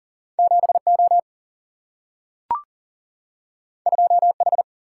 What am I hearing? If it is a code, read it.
Morse code, heard as a keyed tone, about 39 wpm, 700 Hz. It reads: 7 O 2 H
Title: Single-Letter Single-Number 40wpm